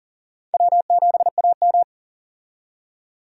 Morse code, heard as a keyed tone, about 40 wpm, 700 Hz. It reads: W7AM
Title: USA Call Signs - 4 Characters - 40wpm